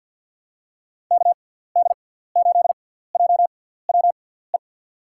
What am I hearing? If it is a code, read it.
Morse code, heard as a keyed tone, about 47 wpm, spaced as for 17 wpm, 700 Hz. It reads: KD8JWE
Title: USA Call Signs - 6 Characters - 6x Character Spacing - 50wpm